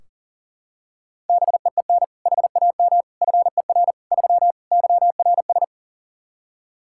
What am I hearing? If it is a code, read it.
Morse code, heard as a keyed tone, about 40 wpm, 700 Hz. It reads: BEEN HAM FER 3 YRS
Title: Practice QSO - Exchange 3 - More Info - 40wpm